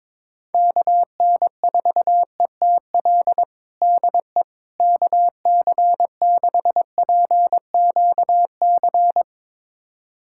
Morse code, heard as a keyed tone, about 21 wpm, 700 Hz. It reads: KN4ETL DE KC6PQC
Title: USA Callsign DE Callsign 22wpm